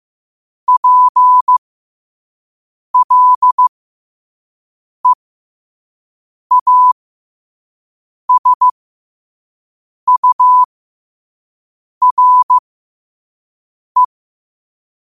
Morse code, heard as a keyed tone, about 15 wpm, spaced as for 5 wpm, 1000 Hz. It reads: PLEASURE
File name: 5wpm